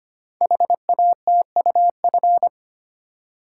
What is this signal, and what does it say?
Morse code, heard as a keyed tone, about 25 wpm, 700 Hz. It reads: HATUF
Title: Code Groups - 5 Letters - 25wpm